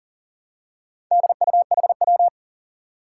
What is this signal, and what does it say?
Morse code, heard as a keyed tone, about 40 wpm, 700 Hz. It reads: DUHW
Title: Four Challenging Letter-Number Combinations 40wpm